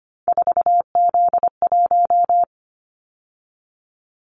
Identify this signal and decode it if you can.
Morse code, heard as a keyed tone, about 25 wpm, 700 Hz. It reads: 4Z1
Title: Callsign Prefixes for Top 50 DXCC - 25wpm